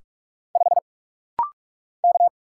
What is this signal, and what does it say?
Morse code, heard as a keyed tone, about 44 wpm, 700 Hz. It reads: 5 K
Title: Single Letter-Number - Rapid-Fire 45wpm